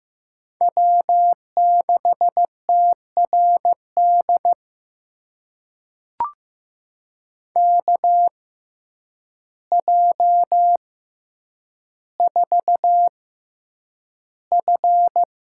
Morse code, heard as a keyed tone, about 15 wpm, spaced as for 8 wpm, 700 Hz. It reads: W6TRD KJ4F
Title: USA Call Signs - 6x Character Spacing - 15wpm